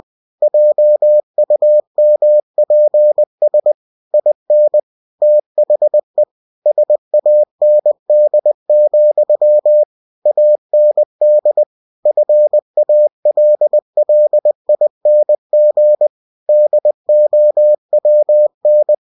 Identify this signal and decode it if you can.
Morse code, heard as a keyed tone, about 20 wpm, 600 Hz. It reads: JUMPS IN THE SAND, AND FALLING DOWN
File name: cw-books / winnie-the-pooh / winnie-20wpm-page-0066